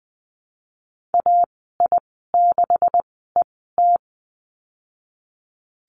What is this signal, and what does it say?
Morse code, heard as a keyed tone, about 20 wpm, spaced as for 15 wpm, 700 Hz. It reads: AI6ET
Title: 173 Lesson 28g - Number 9 - Call Signs 20wpm